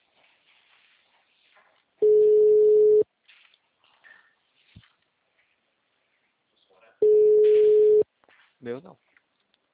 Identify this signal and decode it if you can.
Ringback tone (Europe, 425 Hz), with speech.